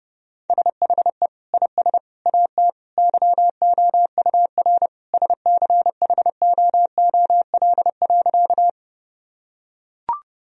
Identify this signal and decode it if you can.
Morse code, heard as a keyed tone, about 30 wpm, 700 Hz. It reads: SHE IS AT YOUR SCHOOL.
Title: Sentences from Top 1000 Words - Review Part 1 - 30wpm